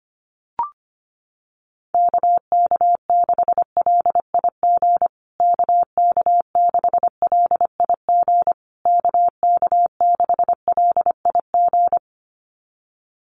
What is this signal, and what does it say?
Morse code, heard as a keyed tone, about 25 wpm, 700 Hz. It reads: KK6LIG KK6LIG KK6LIG
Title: USA Call Signs Repeated 3x 25wpm